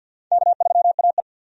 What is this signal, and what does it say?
Morse code, heard as a keyed tone, about 47 wpm, 700 Hz. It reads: K3AE
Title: USA Call Signs - 4 Characters - 50wpm